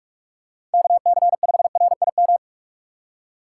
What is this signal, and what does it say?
Morse code, heard as a keyed tone, about 42 wpm, 700 Hz. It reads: KC5RIM
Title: USA Call Signs - 6 Characters - 4x Character Spacing - 45wpm